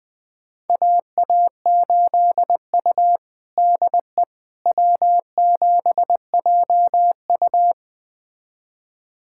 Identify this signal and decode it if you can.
Morse code, heard as a keyed tone, about 21 wpm, 700 Hz. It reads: AA8U DE W7JU
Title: USA Callsign DE Callsign 20wpm